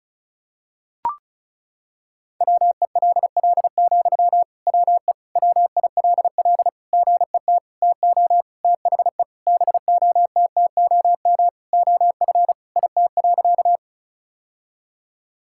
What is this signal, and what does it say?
Morse code, heard as a keyed tone, about 36 wpm, 700 Hz. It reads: WELL, WE WILL GET TO THE BOTTOM OF IT.
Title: Sentences from Top 1000 Words - Review Part 6 - 35wpm